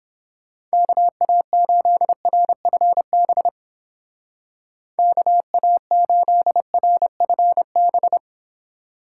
Morse code, heard as a keyed tone, about 29 wpm, 700 Hz. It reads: KA8RFB KA8RFB